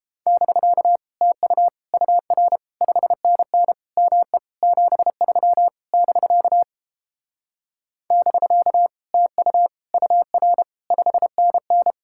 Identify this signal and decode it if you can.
Morse code, heard as a keyed tone, about 29 wpm, 700 Hz. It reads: <BK> TU UR 5NN ME 73 <BK> <BK> TU UR 5NN